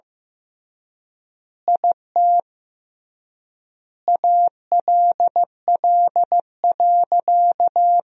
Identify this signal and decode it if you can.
Morse code, heard as a keyed tone, about 15 wpm, 700 Hz. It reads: IT ALL.
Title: 3x Word-Spacing - Sentences from Top 300 Words - 15wpm